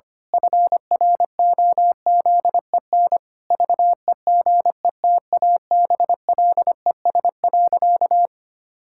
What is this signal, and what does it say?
Morse code, heard as a keyed tone, about 25 wpm, 700 Hz. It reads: FROZEN VEGETABLES.
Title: Pangram Challenge v2 - 25wpm